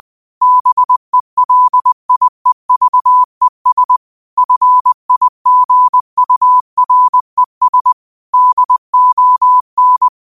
Morse code, heard as a keyed tone, about 20 wpm, 1000 Hz. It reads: BELIEVES FIGURES DON
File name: qotdfiles / 20wpm